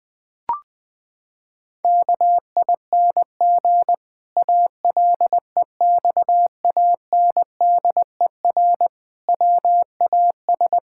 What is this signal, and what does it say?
Morse code, heard as a keyed tone, about 20 wpm, 700 Hz. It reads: KING ALEXANDER WAS
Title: Pangram Challenge v2 - 20wpm